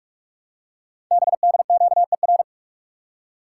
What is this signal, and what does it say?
Morse code, heard as a keyed tone, about 44 wpm, 700 Hz. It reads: BDQER
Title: Code Groups - 5 Letters - 45wpm